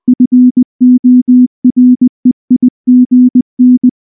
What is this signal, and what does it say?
Morse code, heard as a keyed tone, about 19 wpm, 250 Hz. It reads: FOREIGN